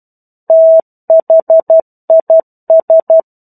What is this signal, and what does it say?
Morse code, heard as a keyed tone, about 12 wpm, 650 Hz. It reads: THIS
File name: 01-Chapter-01 12 WPM